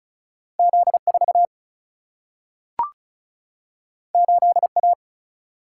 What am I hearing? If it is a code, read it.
Morse code, heard as a keyed tone, about 36 wpm, 700 Hz. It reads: Z4 8A
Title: Single-Letter Single-Number 35wpm